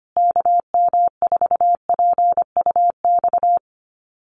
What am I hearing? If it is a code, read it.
Morse code, heard as a keyed tone, about 25 wpm, 700 Hz. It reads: KM4PUX
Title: USA Call Signs Repeated 3x 25wpm